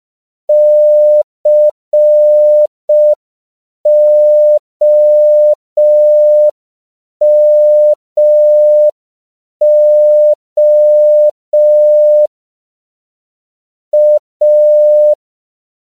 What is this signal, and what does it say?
Morse code, heard as a keyed tone, about 5 wpm, 600 Hz. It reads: COMO A
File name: CW / 5PPM